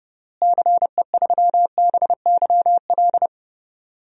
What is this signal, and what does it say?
Morse code, heard as a keyed tone, about 32 wpm, 700 Hz. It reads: CE3BYL
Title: DXCC-100 Call Signs - 6 Characters - Repeated 3x - 30wpm